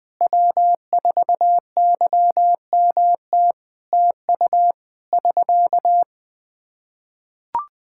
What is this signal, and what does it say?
Morse code, heard as a keyed tone, about 21 wpm, 700 Hz. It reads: W4YMT TU <SK>
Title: Practice QSO - Exchange 4 - Ending - 20wpm